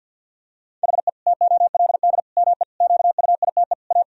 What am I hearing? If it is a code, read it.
Morse code, heard as a keyed tone, about 50 wpm, 700 Hz. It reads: HE TOLD ME QUITE A
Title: Sentences from Top 1000 Words - Review Part 3 - 50wpm